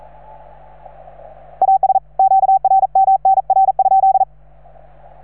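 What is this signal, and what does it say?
Morse code, heard as a keyed tone, about 40 wpm, 770 Hz. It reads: AS QRM NR?
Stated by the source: Morse from station 3V9A.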